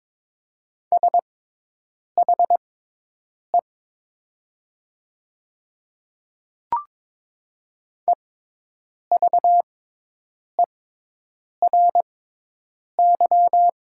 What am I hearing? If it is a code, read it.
Morse code, heard as a keyed tone, about 22 wpm, spaced as for 8 wpm, 700 Hz. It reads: SHE EVERY
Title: 6x Character Spacing - Top 200 Words - 22wpm